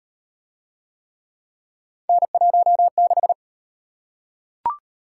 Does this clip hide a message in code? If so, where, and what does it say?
Yes: Morse code from 2.1 to 3.3 s, heard as a keyed tone, about 38 wpm, 700 Hz. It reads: N16